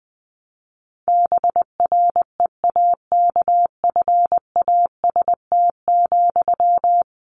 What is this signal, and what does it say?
Morse code, heard as a keyed tone, about 20 wpm, 700 Hz. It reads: BREAKFAST,
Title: Sentences from Top 1500 Words 20wpm